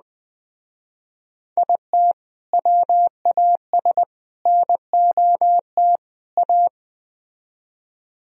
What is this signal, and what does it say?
Morse code, heard as a keyed tone, about 20 wpm, 700 Hz. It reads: IT WAS NOT A
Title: Speed-Racing - Sets of 4 Words 17wpm